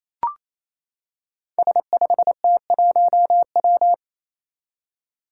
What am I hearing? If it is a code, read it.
Morse code, heard as a keyed tone, about 28 wpm, 700 Hz. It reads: S5T1W
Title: Five Challenging Letter-Number Combinations 28wpm